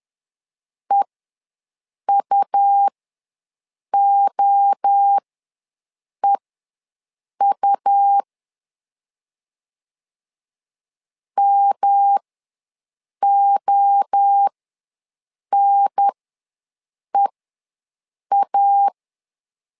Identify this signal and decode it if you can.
Morse code, heard as a keyed tone, about 10 wpm, spaced as for 6 wpm, 800 Hz. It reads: EUOEU MONEA
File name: LEZIONI / 3 MWRO / 3 MWRO ws 10wpm 2310